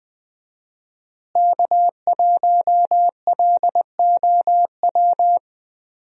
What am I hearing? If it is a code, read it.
Morse code, heard as a keyed tone, about 20 wpm, 700 Hz. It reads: K1LOW
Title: USA Call Signs - 5 Characters - Repeated 3x - 20wpm